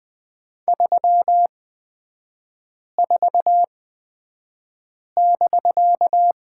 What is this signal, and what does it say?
Morse code, heard as a keyed tone, about 20 wpm, spaced as for 5 wpm, 700 Hz. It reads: 34<BK>
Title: 250 Lesson 40e - BK - Three Characters 20wpm